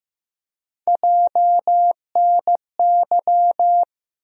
Morse code, heard as a keyed tone, about 15 wpm, 700 Hz. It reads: JNY
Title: USA Call Sign Postfixes - 3 Letters 15wpm